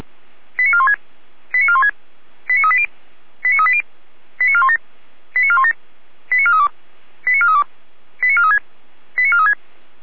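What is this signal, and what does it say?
Five-tone selective call (ZVEI), heard as quick runs of five tones, tones of about 70 ms. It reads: 89416 89416 89380 89380 89417 89417 89432 89432 89436 89436
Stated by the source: ZVEI five-tone selective call received by SDR.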